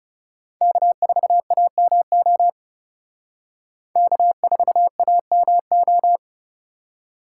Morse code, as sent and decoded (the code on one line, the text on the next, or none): -.- ....- .- -- --- / -.- ....- .- -- ---
K4AMO K4AMO